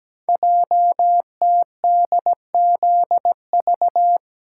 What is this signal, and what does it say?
Morse code, heard as a keyed tone, about 17 wpm, 700 Hz. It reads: JTDZV
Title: Code Groups - 5 Letters - 17wpm